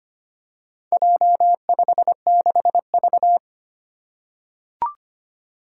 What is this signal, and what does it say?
Morse code, heard as a keyed tone, about 25 wpm, 700 Hz. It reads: J56V
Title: Four Challenging Letter-Number Combinations 25wpm